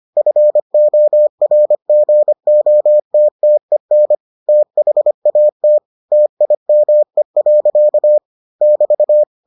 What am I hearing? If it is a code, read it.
Morse code, heard as a keyed tone, about 25 wpm, 600 Hz. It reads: FORGOTTEN THAT TIME. <BT>